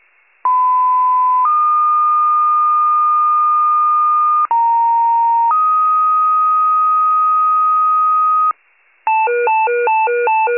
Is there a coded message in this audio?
Yes: two-tone page from 0.4 to 8.5 s, heard as a one-second tone, then a three-second tone, about 1010 then 1220 Hz, about 930 then 1250 Hz.